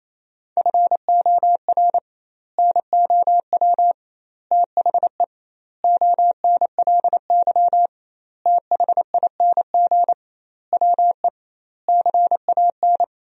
Morse code, heard as a keyed tone, about 28 wpm, 700 Hz. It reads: FOR NOW THE ONLY THING WE CAN
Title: 2x Word-Spacing - Sentences from Top 200 Words - 28wpm